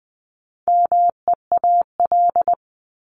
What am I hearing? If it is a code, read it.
Morse code, heard as a keyed tone, about 20 wpm, 700 Hz. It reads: MEAL
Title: Top 1751-2000 Words 20wpm